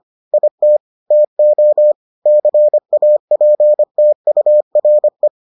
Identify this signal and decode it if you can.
Morse code, heard as a keyed tone, about 25 wpm, 600 Hz. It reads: IT TO CAPTURE